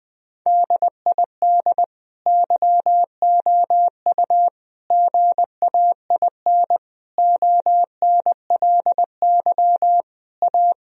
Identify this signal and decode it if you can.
Morse code, heard as a keyed tone, about 21 wpm, 700 Hz. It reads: DID YOU GAIN ONLY A
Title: Sentences from Top 1000 Words - Review Part 2 - 20wpm